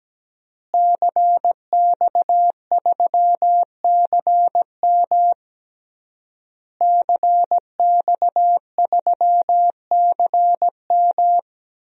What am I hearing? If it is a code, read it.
Morse code, heard as a keyed tone, about 17 wpm, 700 Hz. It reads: CX3CM CX3CM
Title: DXCC-100 Call Signs - 5 Characters - Repeated 3x - 17wpm